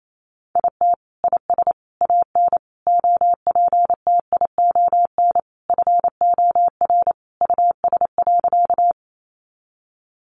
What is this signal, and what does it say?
Morse code, heard as a keyed tone, about 28 wpm, 700 Hz. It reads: IT IS AN OPTION FOR US.
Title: Sentences from Top 1300 Words 28wpm